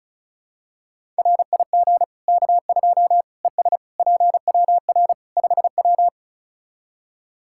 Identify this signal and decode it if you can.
Morse code, heard as a keyed tone, about 34 wpm, 700 Hz. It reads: RIG K2 ES PWR 5W
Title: Practice QSO - Exchange 2 - Info - 35wpm